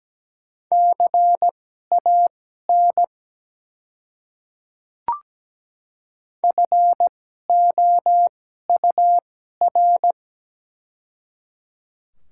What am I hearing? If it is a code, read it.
Morse code, heard as a keyed tone, about 17 wpm, spaced as for 12 wpm, 700 Hz. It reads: CAN FOUR
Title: 2x Character Spacing - Top 300 Words - 17wpm